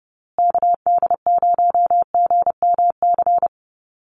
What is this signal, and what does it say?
Morse code, heard as a keyed tone, about 30 wpm, 700 Hz. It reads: KD0GMC
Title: USA Call Signs Repeated 3x 30wpm